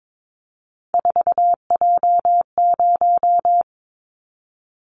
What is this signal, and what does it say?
Morse code, heard as a keyed tone, about 22 wpm, 700 Hz. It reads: 4J0
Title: Callsign Prefixes for Top 100 DXCC - 22wpm